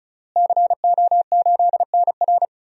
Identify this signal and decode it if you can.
Morse code, heard as a keyed tone, about 36 wpm, 700 Hz. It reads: CO8NR
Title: DXCC-100 Call Signs - 5 Characters - Repeated 3x - 35wpm